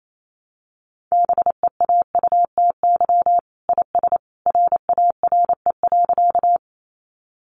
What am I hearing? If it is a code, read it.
Morse code, heard as a keyed tone, about 28 wpm, 700 Hz. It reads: BEAUTY IS RARE.
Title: Sentences from Top 1100 Words 28wpm